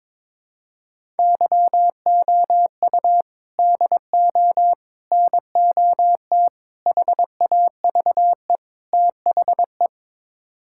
Morse code, heard as a keyed tone, about 22 wpm, 700 Hz. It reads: YOU DO NOT HAVE THE
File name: practice / 22wpm / Build A Phrase - 5-Word Phrases - 22wpm